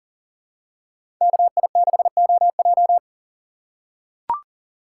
Morse code, heard as a keyed tone, about 40 wpm, 700 Hz. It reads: KI6OJ
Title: USA Call Signs - 5 Characters - 3x Character Spacing - 40wpm